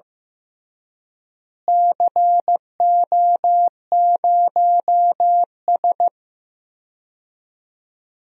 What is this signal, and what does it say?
Morse code, heard as a keyed tone, about 15 wpm, 700 Hz. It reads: CO0S
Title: DXCC-100 Call Signs - 4 Characters - Repeated 3x - 15wpm